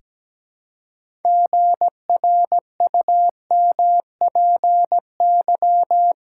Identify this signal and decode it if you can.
Morse code, heard as a keyed tone, about 17 wpm, 700 Hz. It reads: GRUMPY
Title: Pangram Challenge v2 - 17wpm